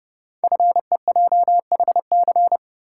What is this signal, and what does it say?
Morse code, heard as a keyed tone, about 30 wpm, 700 Hz. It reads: FEJHC
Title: Code Groups - 5 Letters - 30wpm